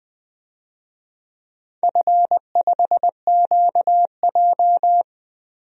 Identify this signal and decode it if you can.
Morse code, heard as a keyed tone, about 20 wpm, 700 Hz. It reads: F5QJ
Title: DXCC-100 Call Signs - 4 Characters - Repeated 3x - 20wpm